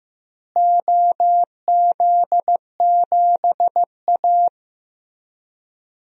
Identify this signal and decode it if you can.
Morse code, heard as a keyed tone, about 16 wpm, 700 Hz. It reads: OZ7A
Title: DXCC-100 Call Signs - 4 Characters - Repeated 3x - 15wpm